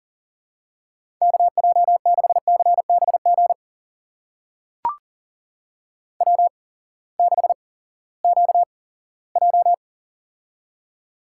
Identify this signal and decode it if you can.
Morse code, heard as a keyed tone, about 40 wpm, spaced as for 20 wpm, 700 Hz. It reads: KJ6CBG W6QJ
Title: USA Call Signs - 8x Character Spacing - 40wpm